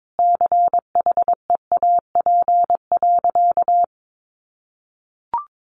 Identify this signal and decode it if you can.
Morse code, heard as a keyed tone, about 22 wpm, 700 Hz. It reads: CHEAP.
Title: Sentences from Top 1200 Words 22wpm